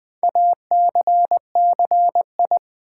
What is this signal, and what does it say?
Morse code, heard as a keyed tone, about 21 wpm, 700 Hz. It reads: ACCI
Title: Build A Word - 4 Letters At A Time - 12-Letter Words - 20wpm